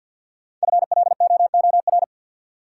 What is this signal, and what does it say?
Morse code, heard as a keyed tone, about 53 wpm, 700 Hz. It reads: FLOOR
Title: Sets of 3 Words - New Word Drill - 50wpm